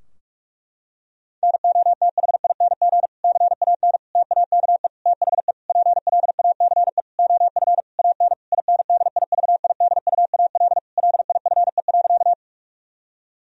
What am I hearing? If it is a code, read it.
Morse code, heard as a keyed tone, about 46 wpm, 700 Hz. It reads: NOTHING CAN TAKE THE PLACE OF AN INDIVIDUAL LIFE.
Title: Sentences from Top 1000 Words - Review Part 4 - 45wpm